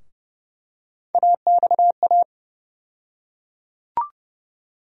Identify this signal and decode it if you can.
Morse code, heard as a keyed tone, about 30 wpm, 700 Hz. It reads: AXA
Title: USA Call Sign Postfixes - 3 Letters 30wpm